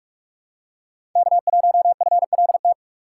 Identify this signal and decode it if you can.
Morse code, heard as a keyed tone, about 43 wpm, 700 Hz. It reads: K1FLT
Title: USA Call Signs - 5 Characters - 2x Character Spacing - 45wpm